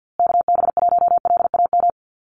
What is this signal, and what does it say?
Morse code, heard as a keyed tone, about 50 wpm, 700 Hz. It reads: KB1LAM
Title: USA Call Signs Repeated 3x 50wpm